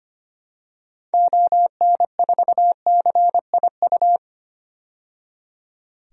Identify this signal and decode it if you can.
Morse code, heard as a keyed tone, about 25 wpm, 700 Hz. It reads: ON4CIU
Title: DXCC-100 Call Signs - 6 Characters - 25wpm